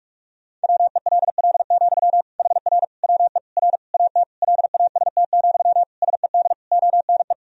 Morse code, heard as a keyed tone, about 41 wpm, 700 Hz. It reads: WELL, HR WE R AT LAST, SED ONE
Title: Sentences from Top 300 Words and Abbreviations 45wpm